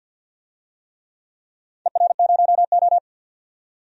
Morse code, heard as a keyed tone, about 52 wpm, 700 Hz. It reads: ER0O
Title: DXCC-100 Call Signs - 4 Characters - 50wpm